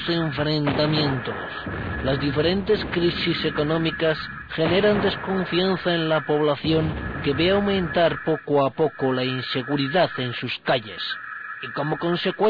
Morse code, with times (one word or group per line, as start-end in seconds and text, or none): none